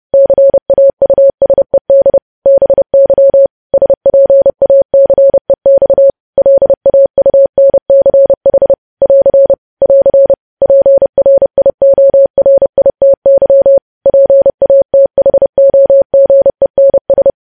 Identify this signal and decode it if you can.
Morse code, heard as a keyed tone, about 30 wpm, 550 Hz. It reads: CAUSED BY SPACEX LAUNCH <AR> <AR> PRIORITY PATHOGENS
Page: news headlines in morse code at 30 wpm